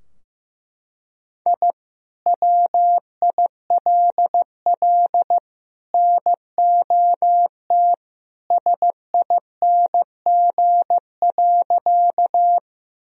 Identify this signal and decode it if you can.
Morse code, heard as a keyed tone, about 15 wpm, 700 Hz. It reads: I WILL NOT SING.
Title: Sentences from Top 1000 Words - Review Part 2 - 15wpm